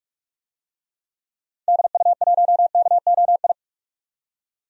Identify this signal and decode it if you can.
Morse code, heard as a keyed tone, about 41 wpm, 690 Hz. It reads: DU1KOI
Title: DXCC-100 Call Signs - 6 Characters - 45wpm